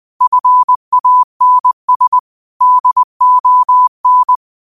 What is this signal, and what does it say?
Morse code, heard as a keyed tone, about 20 wpm, 1000 Hz. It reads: FANS DON
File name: qotdfiles / 20wpm